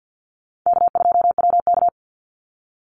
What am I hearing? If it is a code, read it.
Morse code, heard as a keyed tone, about 50 wpm, 700 Hz. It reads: K2WK
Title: Easy Call Signs - 50wpm